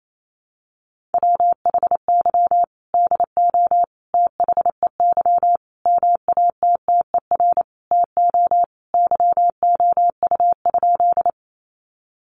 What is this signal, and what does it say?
Morse code, heard as a keyed tone, about 28 wpm, 700 Hz. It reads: WHY DO THEY MATTER TO YOU?
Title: Sentences from Top 700 Words 28wpm